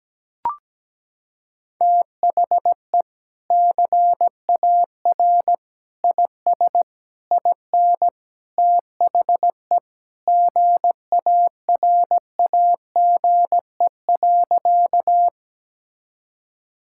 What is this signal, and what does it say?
Morse code, heard as a keyed tone, about 17 wpm, 700 Hz. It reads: THE CAR IS IN THE GARAGE.
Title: Sentences from Top 1600 Words 17wpm